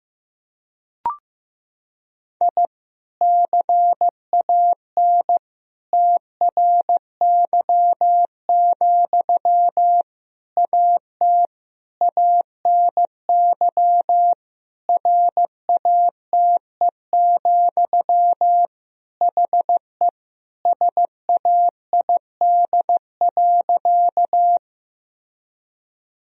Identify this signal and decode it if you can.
Morse code, heard as a keyed tone, about 15 wpm, 700 Hz. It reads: I CAN TRY, AT ANY RATE, HE SAID.
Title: Sentences from Top 1000 Words - Review Part 4 - 15wpm